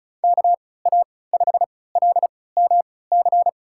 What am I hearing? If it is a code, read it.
Morse code, heard as a keyed tone, about 34 wpm, spaced as for 20 wpm, 700 Hz. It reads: KA5LMC
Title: USA Call Signs - 6 Characters - 3x Character Spacing - 35wpm